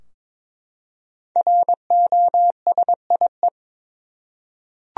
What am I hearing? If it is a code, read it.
Morse code, heard as a keyed tone, about 22 wpm, 700 Hz. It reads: ROSIE
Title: USA Common Names - Max 5 Letters - 22wpm